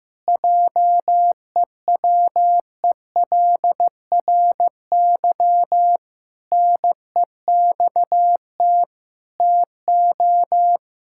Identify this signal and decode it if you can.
Morse code, heard as a keyed tone, about 15 wpm, 700 Hz. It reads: JEWELRY NEXT TO
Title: Pangram Challenge v2 - 15wpm